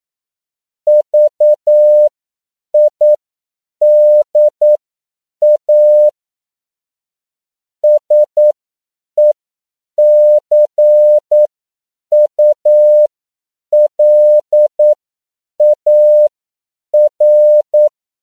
Morse code, heard as a keyed tone, about 9 wpm, spaced as for 7 wpm, 600 Hz. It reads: VIDA SECULAR
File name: CW / 10PPM